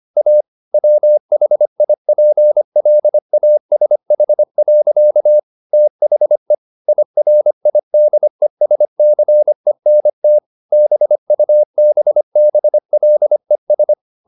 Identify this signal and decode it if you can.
Morse code, heard as a keyed tone, about 25 wpm, 600 Hz. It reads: A WHIPLASH. THE IRIDESCENT BUBBLES
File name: the old man and the sea part 1 25wpm